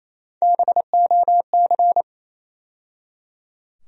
Morse code, heard as a keyed tone, about 27 wpm, 700 Hz. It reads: BOC
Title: USA Call Sign Postfixes - 3 Letters 28wpm